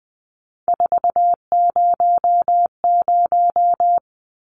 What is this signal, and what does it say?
Morse code, heard as a keyed tone, about 20 wpm, 700 Hz. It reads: 400
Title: Numbers - 3-Digit 20wpm